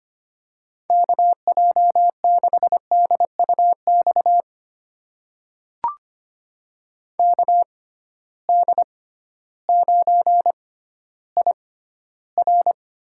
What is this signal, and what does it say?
Morse code, heard as a keyed tone, about 25 wpm, spaced as for 13 wpm, 700 Hz. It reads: KJ6DUX KD9IR
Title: USA Call Signs - 6 Characters - 6x Character Spacing - 25wpm